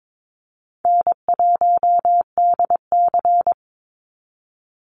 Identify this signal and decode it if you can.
Morse code, heard as a keyed tone, about 22 wpm, 700 Hz. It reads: N1DC
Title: Easy Call Signs - 22wpm